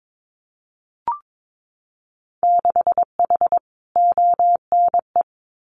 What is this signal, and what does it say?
Morse code, heard as a keyed tone, about 22 wpm, 700 Hz. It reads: 6H ONE
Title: Winter Field Day Exchanges Encore 22wpm